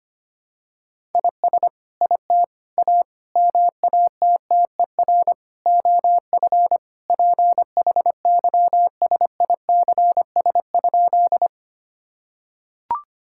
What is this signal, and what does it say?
Morse code, heard as a keyed tone, about 25 wpm, 700 Hz. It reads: IS IT A MATTER OF PHYSICS?
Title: Sentences from Top 1000 Words - Review Part 5 - 25wpm